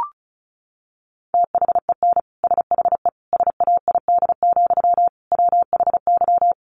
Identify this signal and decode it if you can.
Morse code, heard as a keyed tone, about 34 wpm, 700 Hz. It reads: THEN SHE SAID, WHY
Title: Sentences from Top 800 Words 35wpm